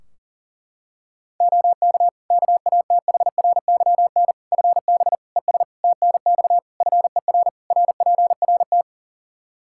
Morse code, heard as a keyed tone, about 40 wpm, 700 Hz. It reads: OK KATHRYN FB ES TNX FER RPRT
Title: Practice QSO - Exchange 2 - Info - 40wpm